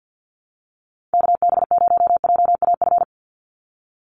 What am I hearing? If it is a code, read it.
Morse code, heard as a keyed tone, about 50 wpm, 700 Hz. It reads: KB0JAF
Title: USA Call Signs 50wpm